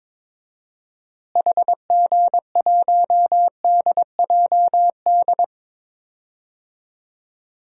Morse code, heard as a keyed tone, about 22 wpm, 700 Hz. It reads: HG1DJD
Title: DXCC-100 Call Signs - 6 Characters - Repeated 3x - 22wpm